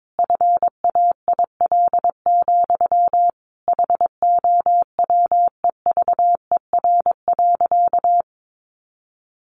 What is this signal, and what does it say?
Morse code, heard as a keyed tone, about 22 wpm, 700 Hz. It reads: FAIL, HOWEVER.
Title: Sentences from Top 1100 Words 22wpm